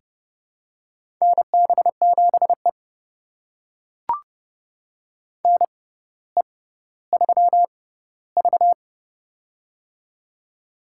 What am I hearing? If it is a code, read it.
Morse code, heard as a keyed tone, about 31 wpm, spaced as for 16 wpm, 700 Hz. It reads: NB7E NE3V
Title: USA Call Signs - 4 Characters - 6x Character Spacing - 30wpm